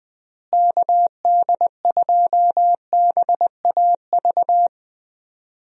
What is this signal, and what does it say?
Morse code, heard as a keyed tone, about 20 wpm, 700 Hz. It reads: KD2BAV
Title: USA Call Signs - 6 Characters - Repeated 3x - 20wpm